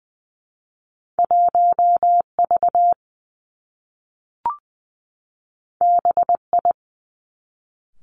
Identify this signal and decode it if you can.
Morse code, heard as a keyed tone, about 20 wpm, 700 Hz. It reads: 1 V B I
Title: Two Challenging Letter-Number Combinations 20wpm